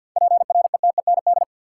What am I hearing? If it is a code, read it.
Morse code, heard as a keyed tone, about 52 wpm, 700 Hz. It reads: PRETEND
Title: Seven Letter Words - 50wpm